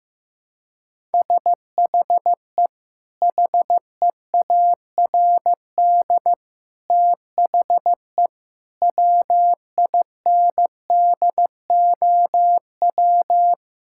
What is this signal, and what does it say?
Morse code, heard as a keyed tone, about 15 wpm, 700 Hz. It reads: SHE HEARD THE WINDOW
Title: Sentences from Top 1600 Words 15wpm